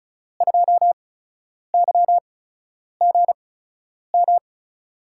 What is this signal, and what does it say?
Morse code, heard as a keyed tone, about 36 wpm, spaced as for 10 wpm, 700 Hz. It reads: 2YGM
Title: USA Call Signs - 6 Characters - 8x Character Spacing - 35wpm